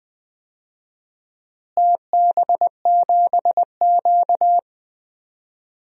Morse code, heard as a keyed tone, about 20 wpm, 700 Hz. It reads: TB7Q
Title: DXCC-100 Call Signs - 4 Characters - Repeated 2x - 20wpm